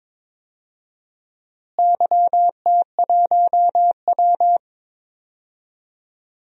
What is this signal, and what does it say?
Morse code, heard as a keyed tone, about 22 wpm, 700 Hz. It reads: YT1W
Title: DXCC-100 Call Signs - 4 Characters - Repeated 2x - 22wpm